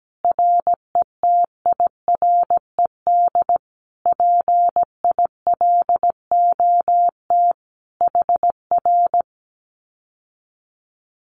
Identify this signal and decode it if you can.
Morse code, heard as a keyed tone, about 17 wpm, 700 Hz. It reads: RETIRED PILOT HR
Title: Common QSO Phrases Encore 17wpm